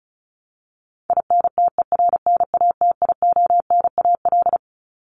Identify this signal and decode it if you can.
Morse code, heard as a keyed tone, about 34 wpm, 700 Hz. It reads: INTERNATIONAL
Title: Top 501-750 Words 35wpm